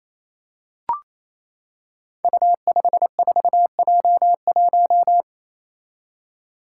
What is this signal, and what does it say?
Morse code, heard as a keyed tone, about 29 wpm, 700 Hz. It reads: U54J1
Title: Five Challenging Letter-Number Combinations 28wpm